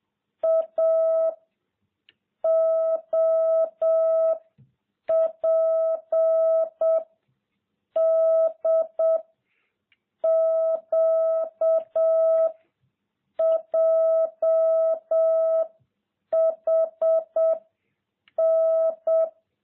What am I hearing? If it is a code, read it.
Morse code, heard as a keyed tone, about 7 wpm, 660 Hz. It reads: A O P D Q J H N